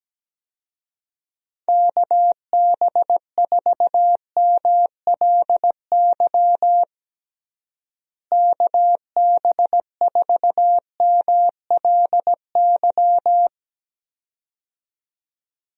Morse code, heard as a keyed tone, about 17 wpm, 700 Hz. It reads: KB4MLY KB4MLY
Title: USA Call Signs - 6 Characters - Repeated 2x - 17wpm